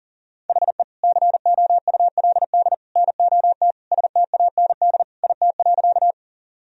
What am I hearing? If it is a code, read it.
Morse code, heard as a keyed tone, about 40 wpm, 700 Hz. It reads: HE COULD NOT STAND IT.
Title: Sentences from Top 1000 Words - Review Part 1 - 40wpm